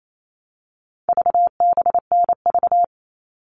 Morse code, heard as a keyed tone, about 28 wpm, 700 Hz. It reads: VBNV
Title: All Cut Numbers - 4-Digit 28wpm